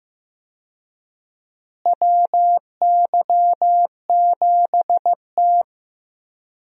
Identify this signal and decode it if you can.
Morse code, heard as a keyed tone, about 15 wpm, 700 Hz. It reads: WY7T
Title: USA Call Signs - 4 Characters - Repeated 2x - 15wpm